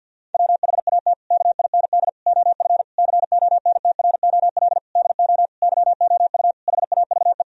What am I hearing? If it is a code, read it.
Morse code, heard as a keyed tone, about 50 wpm, 700 Hz. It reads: WHAT KIND OF CONTROL DO YOU HAVE
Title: Sentences from Top 1000 Words - Review Part 3 - 50wpm